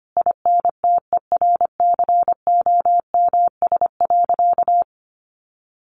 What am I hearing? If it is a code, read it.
Morse code, heard as a keyed tone, about 25 wpm, 700 Hz. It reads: INTERCOMS.
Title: A Sound Of Thunder 25wpm